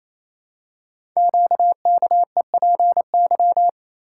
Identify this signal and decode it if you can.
Morse code, heard as a keyed tone, about 27 wpm, 700 Hz. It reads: QKEPY
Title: Code Groups - 5 Letters - 28wpm